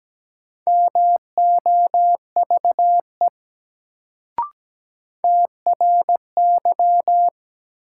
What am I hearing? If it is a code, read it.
Morse code, heard as a keyed tone, about 17 wpm, 700 Hz. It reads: MOVE TRY
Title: Sentences from Top 200 Words - New Word Drill - 17wpm